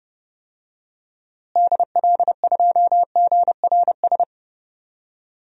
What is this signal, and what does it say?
Morse code, heard as a keyed tone, about 31 wpm, 700 Hz. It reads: DL2GRS
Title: DXCC-100 Call Signs - 6 Characters - Repeated 3x - 30wpm